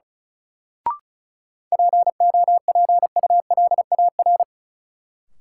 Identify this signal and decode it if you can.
Morse code, heard as a keyed tone, about 36 wpm, 700 Hz. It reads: POPULAR
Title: Sentences from Top 1000 Words - Review Part 3 - New Word Drill - 35wpm